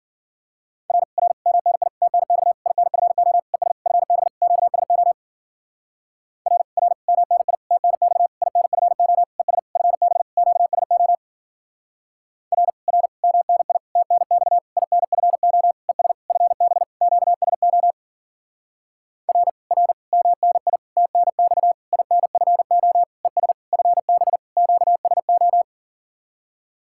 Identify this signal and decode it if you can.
Morse code, heard as a keyed tone, about 52 wpm, 700 Hz. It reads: R R MNI TNX INFO ES FB QSO R R MNI TNX INFO ES FB QSO R R MNI TNX INFO ES FB QSO R R MNI TNX INFO ES FB QSO